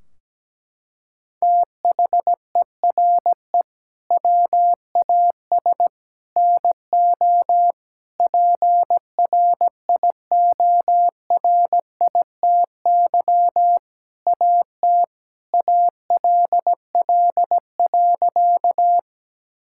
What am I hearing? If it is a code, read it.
Morse code, heard as a keyed tone, about 17 wpm, 700 Hz. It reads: THERE WAS NO PRIORITY AT ALL.
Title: Sentences from Top 1600 Words 17wpm